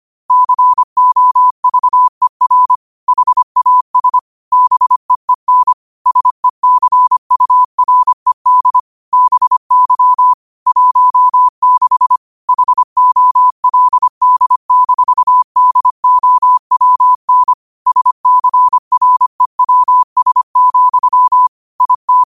Morse code, heard as a keyed tone, about 25 wpm, 1000 Hz. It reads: COVER HAS BEEN SECURED BY 16 HOLD-DOWN SCREWS, IT
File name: qotdfiles / 25wpm